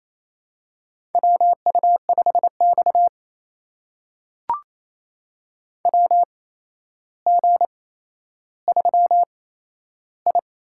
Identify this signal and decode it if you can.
Morse code, heard as a keyed tone, about 27 wpm, spaced as for 12 wpm, 700 Hz. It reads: WU5X WG3I
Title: USA Call Signs - 4 Characters - 8x Character Spacing - 28wpm